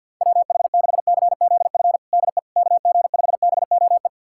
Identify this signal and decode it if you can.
Morse code, heard as a keyed tone, about 48 wpm, 700 Hz. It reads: WH6CZF DE KG5BOE
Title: USA Callsign DE Callsign - 6 Characters - 50wpm